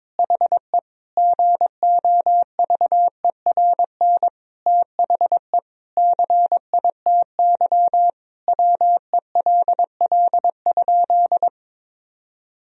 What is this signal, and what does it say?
Morse code, heard as a keyed tone, about 21 wpm, 700 Hz. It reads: HE GOVERN THE CITY WELL?
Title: Sentences from Top 1000 Words - Review Part 2 - 22wpm